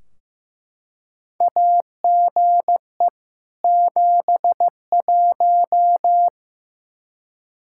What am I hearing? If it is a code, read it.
Morse code, heard as a keyed tone, about 15 wpm, 700 Hz. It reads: AGE 71
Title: Practice QSO - Exchange 3 - More Info - 15wpm